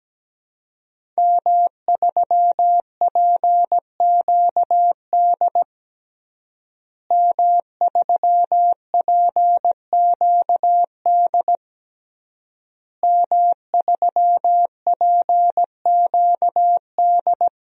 Morse code, heard as a keyed tone, about 17 wpm, 700 Hz. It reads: M3PQD M3PQD M3PQD
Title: DXCC-100 Call Signs - 5 Characters - Repeated 3x - 17wpm